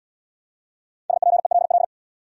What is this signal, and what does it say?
Morse code, heard as a keyed tone, about 69 wpm, 700 Hz. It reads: SPELL